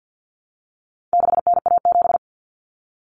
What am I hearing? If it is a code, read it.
Morse code, heard as a keyed tone, about 50 wpm, 700 Hz. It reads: 6NA7
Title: Common Cut Numbers - 4-Digit 50wpm